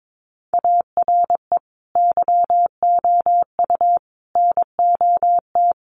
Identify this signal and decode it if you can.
Morse code, heard as a keyed tone, about 22 wpm, 700 Hz. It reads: ARE YOU NOT
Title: Sentences from Top 600 Words 22wpm